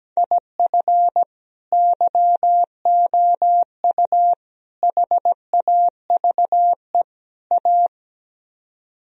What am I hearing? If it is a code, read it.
Morse code, heard as a keyed tone, about 17 wpm, 700 Hz. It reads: IF YOU HAVE A
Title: Speed-Racing - Sets of 4 Words 17wpm